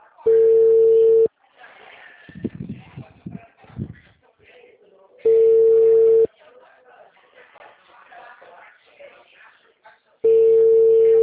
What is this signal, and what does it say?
Ringback tone (China, 450 Hz), with speech.